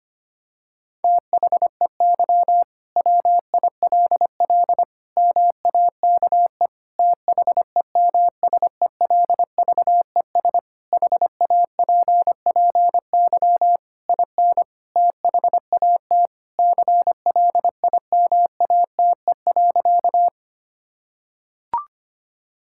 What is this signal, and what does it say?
Morse code, heard as a keyed tone, about 25 wpm, 700 Hz. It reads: THEY WILL MAKE THEMSELVES HAPPY IN THAT CLIMATE.
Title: Sentences from Top 1600 Words 25wpm